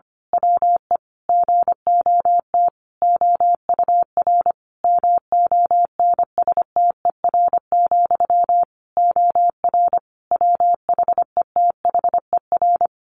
Morse code, heard as a keyed tone, about 25 wpm, 700 Hz. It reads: WE GOT OUR MONSTER, OR WHETHER
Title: A Sound Of Thunder 25wpm